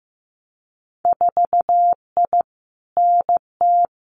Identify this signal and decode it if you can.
Morse code, heard as a keyed tone, about 15 wpm, 700 Hz. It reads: 4I NT
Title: Winter Field Day Exchanges Encore 15wpm